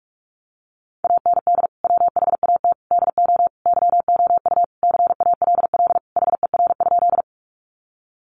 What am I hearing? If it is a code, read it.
Morse code, heard as a keyed tone, about 44 wpm, 700 Hz. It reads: AND WHAT DO YOU CALL HER?
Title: Sentences from Top 100 Words 45wpm